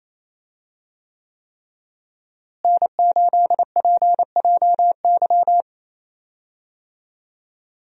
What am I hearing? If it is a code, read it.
Morse code, heard as a keyed tone, about 27 wpm, 700 Hz. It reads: N8PJY
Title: USA Call Signs - 5 Characters - Repeated 3x - 28wpm